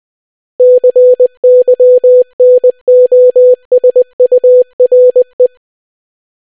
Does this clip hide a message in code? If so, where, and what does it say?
Yes: Morse code from 0.6 to 5.5 s, heard as a keyed tone, about 20 wpm, 500 Hz. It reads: CYNOSURE